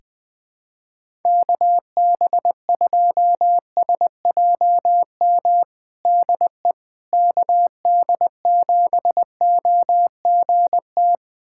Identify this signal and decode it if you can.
Morse code, heard as a keyed tone, about 21 wpm, 700 Hz. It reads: KB2SJM DE KD7OGT
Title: USA Callsign DE Callsign - 6 Characters - 20wpm